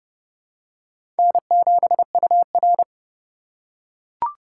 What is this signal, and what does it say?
Morse code, heard as a keyed tone, about 30 wpm, 700 Hz. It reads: N7UR